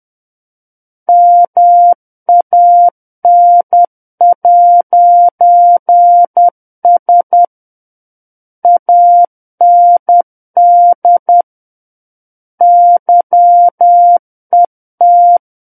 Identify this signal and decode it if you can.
Morse code, heard as a keyed tone, about 10 wpm, 700 Hz. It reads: MAN'S AND YET
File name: Book One / 01-Chapter 1 - 10 WPM